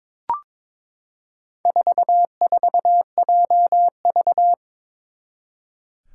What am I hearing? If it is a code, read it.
Morse code, heard as a keyed tone, about 22 wpm, 700 Hz. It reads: 44JV
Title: Four Challenging Letter-Number Combinations 22wpm